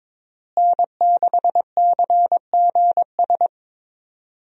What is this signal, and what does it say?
Morse code, heard as a keyed tone, about 22 wpm, 700 Hz. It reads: N6CGS
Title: USA Call Signs - 5 Characters - Repeated 3x - 22wpm